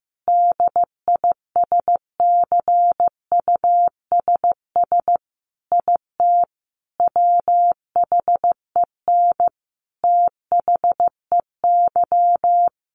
Morse code, heard as a keyed tone, about 14 wpm, 700 Hz. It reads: DISCUSS IT WHEN THEY
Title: Sentences from Top 1400 Words 15wpm